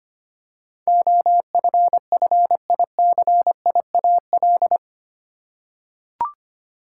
Morse code, heard as a keyed tone, about 25 wpm, 700 Hz. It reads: OFFICIAL
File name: 8 Letter Words - First 3 Letters Given - 25wpm